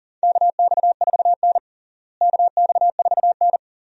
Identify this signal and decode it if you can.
Morse code, heard as a keyed tone, about 40 wpm, 700 Hz. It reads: KX4N KX4N
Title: USA Call Signs - 4 Characters - Repeated 2x - 40wpm